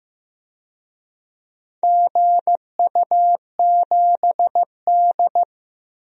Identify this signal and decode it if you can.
Morse code, heard as a keyed tone, about 15 wpm, 700 Hz. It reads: GU7D
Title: DXCC-100 Call Signs - 4 Characters - 15wpm